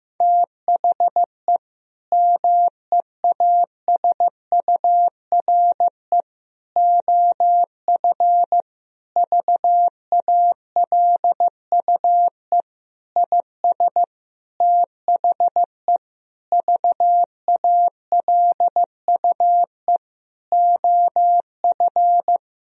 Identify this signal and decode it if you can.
Morse code, heard as a keyed tone, about 14 wpm, 700 Hz. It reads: THE MEASURE OF VALUE IS THE VALUE OF
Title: Sentences from Top 1000 Words - Review Part 3 - 15wpm